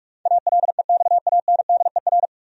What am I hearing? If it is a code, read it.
Morse code, heard as a keyed tone, about 46 wpm, 700 Hz. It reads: ALEXANDER
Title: USA Common Names - 45wpm